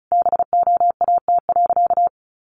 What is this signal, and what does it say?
Morse code, heard as a keyed tone, about 34 wpm, 700 Hz. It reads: BOAT.
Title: Sentences from Top 1500 Words 35wpm